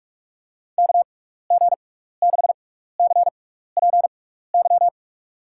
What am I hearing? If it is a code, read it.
Morse code, heard as a keyed tone, about 44 wpm, spaced as for 15 wpm, 700 Hz. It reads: KG6CPY
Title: USA Call Signs - 6x Character Spacing - 45wpm